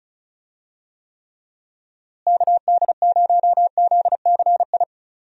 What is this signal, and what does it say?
Morse code, heard as a keyed tone, about 34 wpm, 700 Hz. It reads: KD0ZCI
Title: USA Call Signs - 6 Characters - Repeated 3x - 35wpm